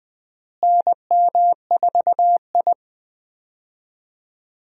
Morse code, heard as a keyed tone, about 20 wpm, 700 Hz. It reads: NM4I
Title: USA Call Signs - 4 Characters - Repeated 3x - 20wpm